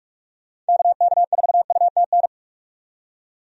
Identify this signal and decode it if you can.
Morse code, heard as a keyed tone, about 46 wpm, 700 Hz. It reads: KK4UTN
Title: USA Call Signs - 6 Characters - 45wpm